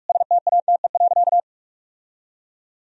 Morse code, heard as a keyed tone, about 46 wpm, 700 Hz. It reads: STATE.
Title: Sentences from Top 1000 Words - Review Part 3 - 45wpm